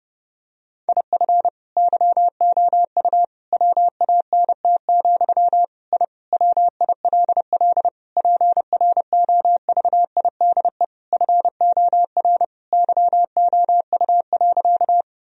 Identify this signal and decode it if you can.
Morse code, heard as a keyed tone, about 30 wpm, 700 Hz. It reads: IF YOU WANT, I WILL PROVIDE FOR YOU.
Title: Sentences from Top 1000 Words - Review Part 3 - 30wpm